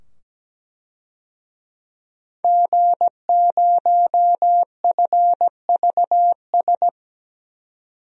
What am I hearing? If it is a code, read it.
Morse code, heard as a keyed tone, about 17 wpm, 700 Hz. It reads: G0FVS
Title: DXCC-100 Call Signs - 5 Characters - 17wpm